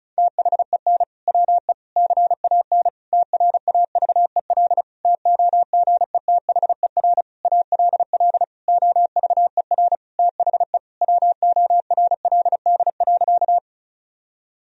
Morse code, heard as a keyed tone, about 36 wpm, 700 Hz. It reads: THEN WE CAN TRAVEL TOGETHER ALL OVER THE WORLD.
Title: Sentences from Top 1000 Words - Review Part 2 - 35wpm